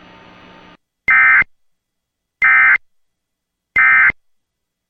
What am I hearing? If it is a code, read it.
Emergency Alert System (SAME) tones, heard as harsh buzzing bursts. It is heard from 1.1 to 4.1 s.